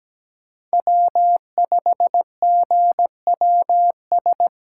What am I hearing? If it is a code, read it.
Morse code, heard as a keyed tone, about 17 wpm, 700 Hz. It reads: W5GWS
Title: USA Call Signs - 5 Characters - 2x Character Spacing - 17wpm